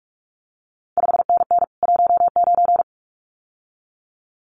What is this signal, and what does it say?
Morse code, heard as a keyed tone, about 46 wpm, 700 Hz. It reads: 5NN 19
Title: CQ WW DX Contest Exchange - 45wpm